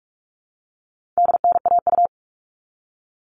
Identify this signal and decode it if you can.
Morse code, heard as a keyed tone, about 44 wpm, 700 Hz. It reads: DNAU
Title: All Cut Numbers - 4-Digit 45wpm